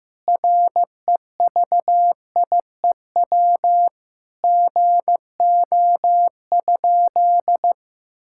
Morse code, heard as a keyed tone, about 15 wpm, 700 Hz. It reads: REVIEW GO?
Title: Sentences from Top 1000 Words - Review Part 6 - 15wpm